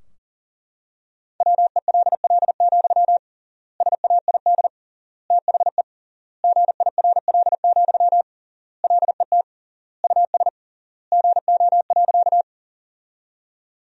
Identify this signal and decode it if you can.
Morse code, heard as a keyed tone, about 40 wpm, 700 Hz. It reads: WELL, SAID THE GIRL, LET US GO.
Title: 3x Word-Spacing - Sentences from Top 400 Words - 40wpm